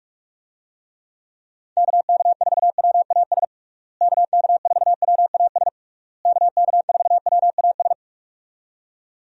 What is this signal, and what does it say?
Morse code, heard as a keyed tone, about 46 wpm, 700 Hz. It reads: KK4WAS KK4WAS KK4WAS
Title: USA Call Signs - 6 Characters - Repeated 3x - 45wpm